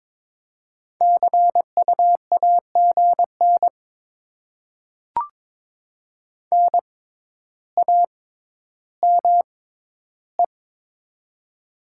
Morse code, heard as a keyed tone, about 22 wpm, spaced as for 11 wpm, 700 Hz. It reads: CU AGN NAME
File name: Top 20 QSO Elements - 6x Character Spacing - 22wpm